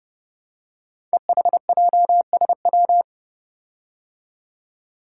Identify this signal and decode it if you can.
Morse code, heard as a keyed tone, about 30 wpm, 700 Hz. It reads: EHJSW
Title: Code Groups - 5 Letters - 30wpm